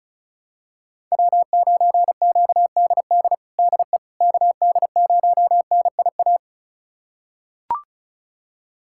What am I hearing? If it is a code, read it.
Morse code, heard as a keyed tone, about 36 wpm, 700 Hz. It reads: W9QDD DE KD0NIA
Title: USA Callsign DE Callsign 35wpm